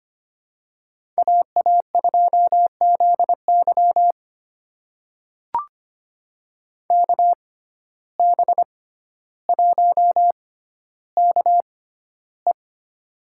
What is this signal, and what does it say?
Morse code, heard as a keyed tone, about 25 wpm, spaced as for 13 wpm, 700 Hz. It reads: AA2ZY KB1KE
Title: USA Call Signs - 5 Characters - 6x Character Spacing - 25wpm